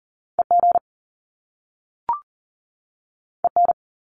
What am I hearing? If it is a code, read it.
Morse code, heard as a keyed tone, about 39 wpm, 700 Hz. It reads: EG EN
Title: Top Two Letter Combinations in Words 40wpm